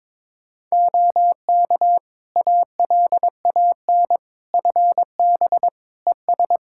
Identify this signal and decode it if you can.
Morse code, heard as a keyed tone, about 22 wpm, 700 Hz. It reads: OK ALAN FB ES
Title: Practice QSO - Exchange 2 - Info - 22wpm